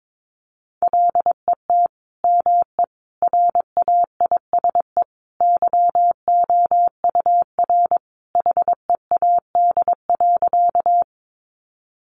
Morse code, heard as a keyed tone, about 22 wpm, 700 Hz. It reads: LET ME RAISE YOUR HEAD.
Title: Sentences from Top 1400 Words 22wpm